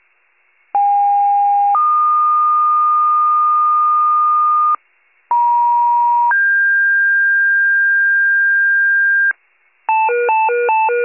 Two-tone page, heard as a one-second tone, then a three-second tone, about 810 then 1220 Hz, about 950 then 1600 Hz, with noise. It is heard from 0.7 to 9.3 s.
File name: pager / 83 RESCUE DUTY / wcalert